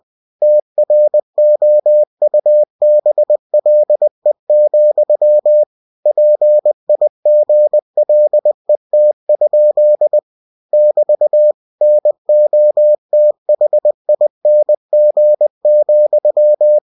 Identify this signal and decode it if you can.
Morse code, heard as a keyed tone, about 20 wpm, 600 Hz. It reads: TROUBLE, PIGLET? <BT> NOTHING,
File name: cw-books / winnie-the-pooh / winnie-20wpm-page-0065